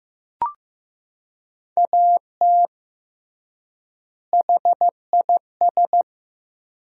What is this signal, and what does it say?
Morse code, heard as a keyed tone, about 16 wpm, 700 Hz. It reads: AT HIS
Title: 3x Word-Spacing - Sentences from Top 500 Words - 15wpm